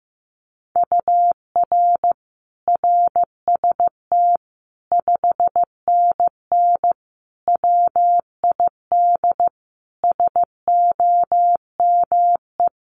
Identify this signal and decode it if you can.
Morse code, heard as a keyed tone, about 15 wpm, 700 Hz. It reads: UR RST 5NN WID SOME
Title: Common QSO Phrases 15wpm